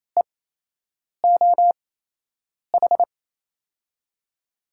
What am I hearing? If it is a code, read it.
Morse code, heard as a keyed tone, about 28 wpm, spaced as for 8 wpm, 700 Hz. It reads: EOH
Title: USA Call Signs - 6 Characters - 8x Character Spacing - 28wpm